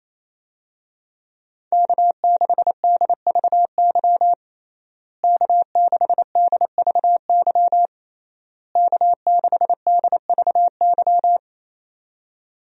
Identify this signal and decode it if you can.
Morse code, heard as a keyed tone, about 27 wpm, 700 Hz. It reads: K6DVY K6DVY K6DVY
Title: USA Call Signs - 5 Characters - Repeated 3x - 28wpm